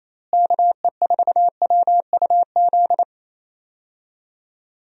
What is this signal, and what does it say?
Morse code, heard as a keyed tone, about 27 wpm, 700 Hz. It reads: KE4WUZ
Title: USA Call Signs - 6 Characters - Repeated 3x - 28wpm